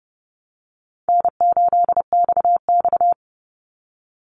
Morse code, heard as a keyed tone, about 30 wpm, 700 Hz. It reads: N8XX
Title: Easy Call Signs Repeated 2x - 30wpm